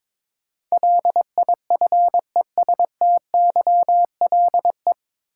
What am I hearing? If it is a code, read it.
Morse code, heard as a keyed tone, about 22 wpm, 700 Hz. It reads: LIFESTYLE
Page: Morse Code Ninja practice 22wpm